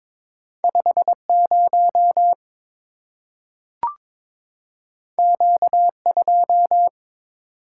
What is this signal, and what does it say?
Morse code, heard as a keyed tone, about 22 wpm, 690 Hz. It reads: 50 Q2
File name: practice / 22wpm / Two Letter-Number Combinations - 22wpm